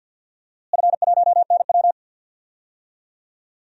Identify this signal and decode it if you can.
Morse code, heard as a keyed tone, about 52 wpm, 700 Hz. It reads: F1NW
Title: DXCC-100 Call Signs - 4 Characters - Repeated 2x - 50wpm